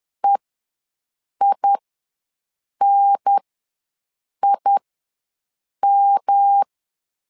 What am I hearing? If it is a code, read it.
Morse code, heard as a keyed tone, about 10 wpm, spaced as for 6 wpm, 800 Hz. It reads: EINIM
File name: LEZIONI / 3 MWRO / 3 M ws 10wpm 2121